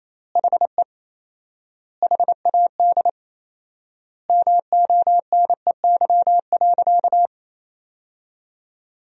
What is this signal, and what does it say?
Morse code, heard as a keyed tone, about 28 wpm, 700 Hz. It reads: HE HAD MONEY.
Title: 4x Word-Spacing - Sentences from Top 500 Words - 28wpm